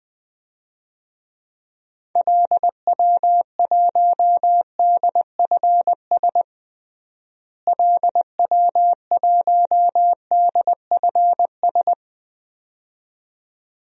Morse code, heard as a keyed tone, about 20 wpm, 700 Hz. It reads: LW1DFS LW1DFS
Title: DXCC-100 Call Signs - 6 Characters - Repeated 2x - 20wpm